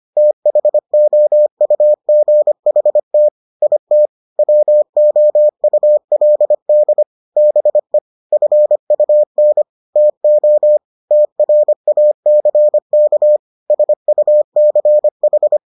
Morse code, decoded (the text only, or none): THOUGHT IT WOULD BE FUN TO TRACK SUCH